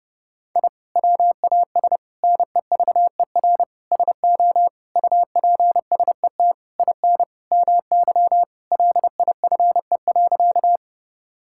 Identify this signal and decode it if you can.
Morse code, heard as a keyed tone, about 30 wpm, 700 Hz. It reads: I WAS NEVER SO UPSET IN MY LIFE.
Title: Sentences from Top 1000 Words - Review Part 5 - 30wpm